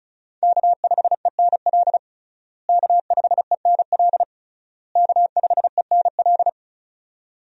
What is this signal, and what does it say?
Morse code, heard as a keyed tone, about 36 wpm, 700 Hz. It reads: K5ENL K5ENL K5ENL
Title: USA Call Signs - 5 Characters - Repeated 3x - 35wpm